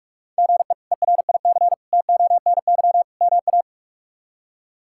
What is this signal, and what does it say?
Morse code, heard as a keyed tone, about 44 wpm, 700 Hz. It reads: GE ERIC TONY MA
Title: K1USN SST QSO Practice - 45wpm